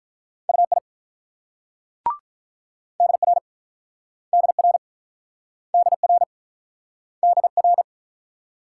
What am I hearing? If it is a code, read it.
Morse code, heard as a keyed tone, about 49 wpm, 700 Hz. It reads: UI DR DR DR DR